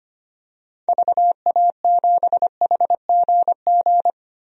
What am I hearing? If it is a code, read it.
Morse code, heard as a keyed tone, about 25 wpm, 700 Hz. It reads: VA7HGG
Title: DXCC-100 Call Signs - 6 Characters - 25wpm